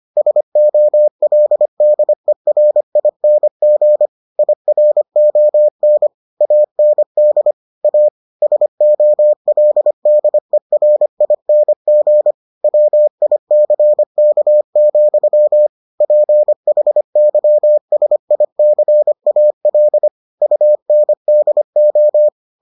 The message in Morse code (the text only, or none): SOLDERING IRON AND A SOLDERING WICK, PHYSICAL UNDO